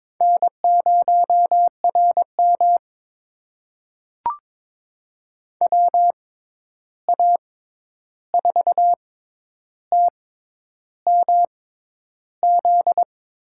Morse code, heard as a keyed tone, about 22 wpm, spaced as for 10 wpm, 700 Hz. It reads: N0RM WA4TMZ
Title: USA Call Signs - 6x Character Spacing - 22wpm